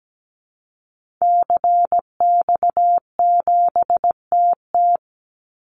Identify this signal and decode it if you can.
Morse code, heard as a keyed tone, about 17 wpm, 700 Hz. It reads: CX7TT
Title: Call Signs Repeated 3x - 17wpm